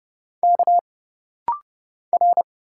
Morse code, heard as a keyed tone, about 30 wpm, 700 Hz. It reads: K R
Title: Single Letter-Number - Rapid-Fire 30wpm